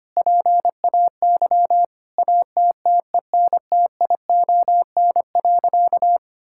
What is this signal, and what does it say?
Morse code, heard as a keyed tone, about 25 wpm, 700 Hz. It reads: PAY ATTENTION.
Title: Sentences from Top 1000 Words - Review Part 6 - 25wpm